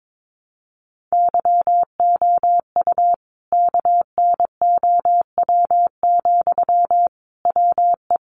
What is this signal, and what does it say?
Morse code, heard as a keyed tone, about 22 wpm, 700 Hz. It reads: YOU KNOW, WE
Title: Sentences from Top 600 Words 22wpm